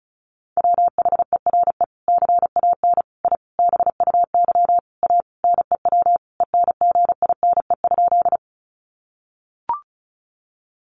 Morse code, heard as a keyed tone, about 34 wpm, 700 Hz. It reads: WHERE CAN I BUY A NEW ENGINE?
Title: Sentences from Top 1500 Words 35wpm